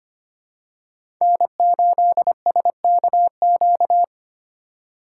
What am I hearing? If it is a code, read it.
Morse code, heard as a keyed tone, about 25 wpm, 700 Hz. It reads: N8SKQ
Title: USA Call Signs - 5 Characters - 4x Character Spacing - 25wpm